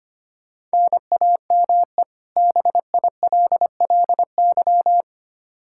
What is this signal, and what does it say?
Morse code, heard as a keyed tone, about 25 wpm, 700 Hz. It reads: NAME BILLY
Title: Practice QSO - Exchange 1 - Beginning - 25wpm